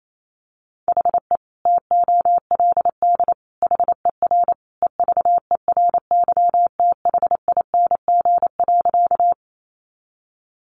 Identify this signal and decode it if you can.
Morse code, heard as a keyed tone, about 28 wpm, 700 Hz. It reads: HE TOLD HER EVERYTHING.
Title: Sentences from Top 700 Words 28wpm